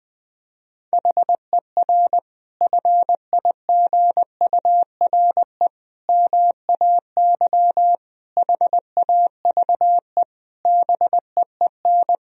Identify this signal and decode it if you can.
Morse code, heard as a keyed tone, about 20 wpm, 700 Hz. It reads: HER FIGURE MAY HAVE BEEN
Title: Sentences from Top 1000 Words - Review Part 2 - 20wpm